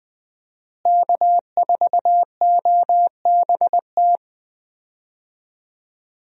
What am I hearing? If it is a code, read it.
Morse code, heard as a keyed tone, about 20 wpm, 700 Hz. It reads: K4OBT
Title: USA Call Signs - 5 Characters - Repeated 3x - 20wpm